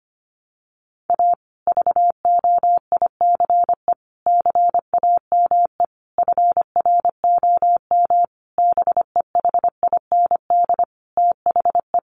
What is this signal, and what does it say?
Morse code, heard as a keyed tone, about 25 wpm, 700 Hz. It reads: A VOICE CAME FROM BEHIND THE
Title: Sentences from Top 1400 Words 25wpm